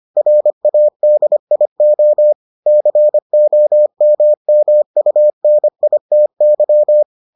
Morse code, heard as a keyed tone, about 25 wpm, 600 Hz. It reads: RADIO COMMUNITY